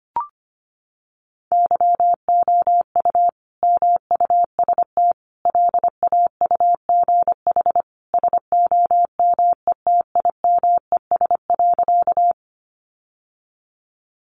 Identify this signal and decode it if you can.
Morse code, heard as a keyed tone, about 25 wpm, 700 Hz. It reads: YOU MUST LAUGH SOMETIMES.
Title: Sentences from Top 600 Words 25wpm